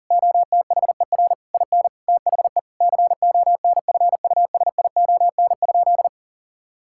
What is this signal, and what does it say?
Morse code, heard as a keyed tone, about 37 wpm, 700 Hz. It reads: OTHER IN THE CONFUSION?
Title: Sentences from Top 1600 Words 40wpm